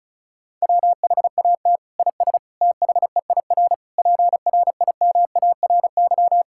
Morse code, heard as a keyed tone, about 36 wpm, 700 Hz. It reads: WHAT IS THEIR PRIMARY
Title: Sentences from Top 1000 Words - Review Part 5 - 35wpm